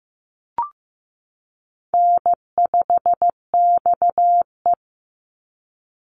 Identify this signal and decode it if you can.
Morse code, heard as a keyed tone, about 15 wpm, 700 Hz. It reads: N5XE
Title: Easy Call Signs Repeated 2x - 15wpm